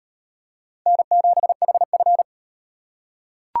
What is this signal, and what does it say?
Morse code, heard as a keyed tone, about 38 wpm, 700 Hz. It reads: N7HF